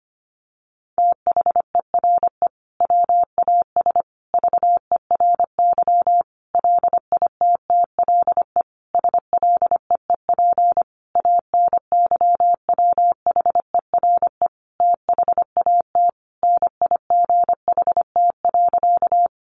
Morse code, heard as a keyed tone, about 25 wpm, 700 Hz. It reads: THERE WAS VERY LITTLE SLEEP ANYWHERE THAT NIGHT.
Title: Sentences from Top 1400 Words 25wpm